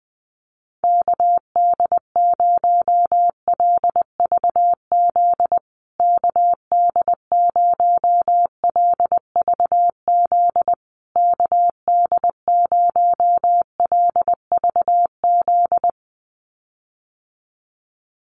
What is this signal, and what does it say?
Morse code, heard as a keyed tone, about 20 wpm, 700 Hz. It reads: KD0LVZ KD0LVZ KD0LVZ
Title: USA Call Signs Repeated 3x 20wpm